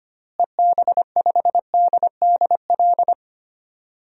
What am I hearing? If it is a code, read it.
Morse code, heard as a keyed tone, about 25 wpm, 700 Hz. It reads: EB5DDL
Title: DXCC-100 Call Signs - 6 Characters - Repeated 2x - 25wpm